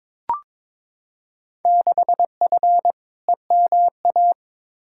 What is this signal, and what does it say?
Morse code, heard as a keyed tone, about 21 wpm, 700 Hz. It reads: 6F EMA
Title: Field Day Exchanges Encore - 22wpm